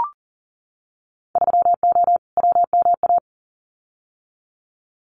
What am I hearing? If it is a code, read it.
Morse code, heard as a keyed tone, about 40 wpm, 700 Hz. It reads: 3O WMA
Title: Winter Field Day Exchanges Encore 40wpm